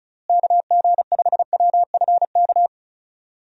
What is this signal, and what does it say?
Morse code, heard as a keyed tone, about 36 wpm, 700 Hz. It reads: KG5WFK
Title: USA Call Signs - 6 Characters - Repeated 2x - 35wpm